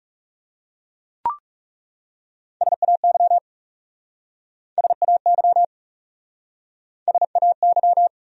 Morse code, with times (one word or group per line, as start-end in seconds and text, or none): 2.6-3.4 SAY
4.8-5.6 SAY
7.1-8.1 SAY